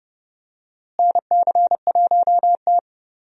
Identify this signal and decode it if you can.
Morse code, heard as a keyed tone, about 30 wpm, 700 Hz. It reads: NC1T
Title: USA Call Signs - 4 Characters - Repeated 3x - 30wpm